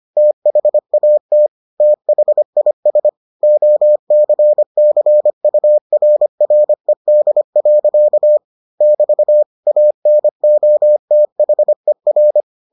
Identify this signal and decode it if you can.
Morse code, heard as a keyed tone, about 25 wpm, 600 Hz. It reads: THAT THIS OCCURRED. <BT> ANOTHER